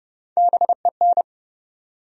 Morse code, heard as a keyed tone, about 30 wpm, 700 Hz. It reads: BEN
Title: USA Common Names - Max 3 Letters - 30wpm